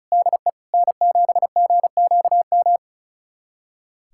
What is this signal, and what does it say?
Morse code, heard as a keyed tone, about 36 wpm, 700 Hz. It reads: DE N7GQM
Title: USA Callsign DE Callsign 35wpm